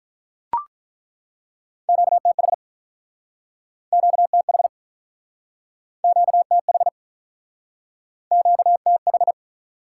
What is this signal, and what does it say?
Morse code, heard as a keyed tone, about 48 wpm, 700 Hz. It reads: QTH QTH QTH QTH